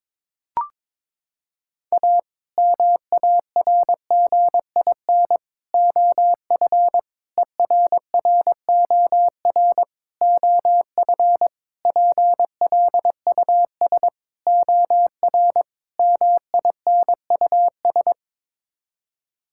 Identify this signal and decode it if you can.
Morse code, heard as a keyed tone, about 21 wpm, 700 Hz. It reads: A MARGIN OF ERROR OF PLUS OR MINUS
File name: practice / 22wpm / Sets of 8 Words - 22wpm